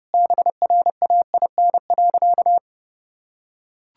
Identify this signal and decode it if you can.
Morse code, heard as a keyed tone, about 30 wpm, 700 Hz. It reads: BRAIN.
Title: Sentences from Top 1600 Words 30wpm